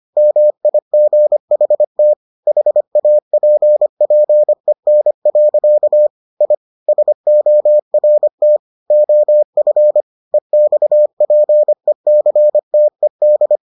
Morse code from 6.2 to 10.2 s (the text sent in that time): I SORT OF